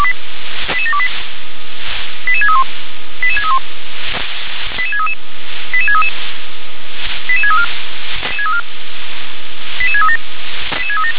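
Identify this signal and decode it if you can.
Five-tone selective call (ZVEI), heard as quick runs of five tones, tones of about 70 ms. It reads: 88728 88631 88631 88633 88633 88634 88637
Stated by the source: ZVEI five-tone call 88731, received on SDR.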